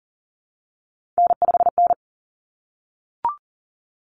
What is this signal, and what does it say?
Morse code, heard as a keyed tone, about 40 wpm, 700 Hz. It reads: N5N
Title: Common Cut Numbers - 3-Digit 40wpm